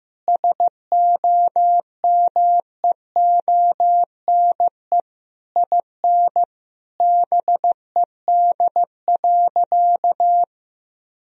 Morse code, heard as a keyed tone, about 15 wpm, 700 Hz. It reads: SOMEONE IN BED.
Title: Sentences from Top 1000 Words - Review Part 4 - 15wpm